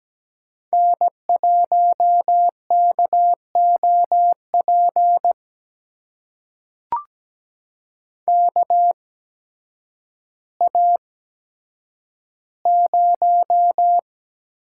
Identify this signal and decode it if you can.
Morse code, heard as a keyed tone, about 17 wpm, spaced as for 9 wpm, 700 Hz. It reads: N1KOP KA0
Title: USA Call Signs - 5 Characters - 8x Character Spacing - 17wpm